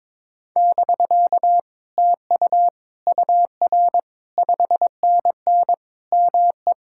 Morse code, heard as a keyed tone, about 22 wpm, 700 Hz. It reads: <BK> TU UR 5NN ME
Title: Speed-Racing Common QSO Phrases Encore - 22wpm